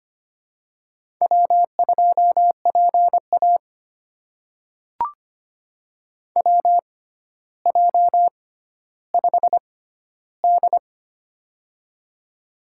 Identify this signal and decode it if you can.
Morse code, heard as a keyed tone, about 25 wpm, spaced as for 13 wpm, 700 Hz. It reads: W2PA WJ5D
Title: USA Call Signs - 4 Characters - 6x Character Spacing - 25wpm